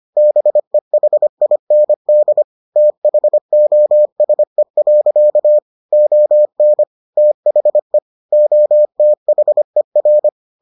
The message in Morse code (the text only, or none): BEHIND THOSE. ON THE OTHER